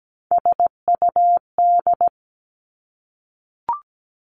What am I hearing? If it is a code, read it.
Morse code, heard as a keyed tone, about 17 wpm, 700 Hz. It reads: SUD
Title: Three Challenging Letter-Number Combinations 17wpm